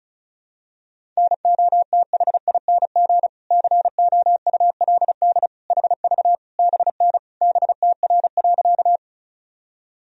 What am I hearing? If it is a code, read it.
Morse code, heard as a keyed tone, about 36 wpm, 700 Hz. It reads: NOTHING COULD HV BN BTR.
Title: Sentences from Top 500 Words and Abbreviations 35wpm